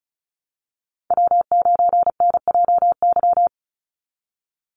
Morse code, heard as a keyed tone, about 34 wpm, 700 Hz. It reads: W9NJY
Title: Call Signs Repeated 3x - 35wpm